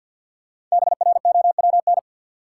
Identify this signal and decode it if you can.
Morse code, heard as a keyed tone, about 53 wpm, 700 Hz. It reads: BROWN
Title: Sentences from Top 1200 Words - New Word Drill - 50wpm